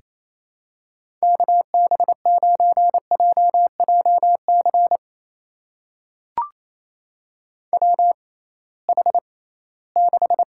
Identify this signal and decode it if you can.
Morse code, heard as a keyed tone, about 28 wpm, spaced as for 18 wpm, 700 Hz. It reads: KB9JJC WH6
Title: USA Call Signs - 6x Character Spacing - 28wpm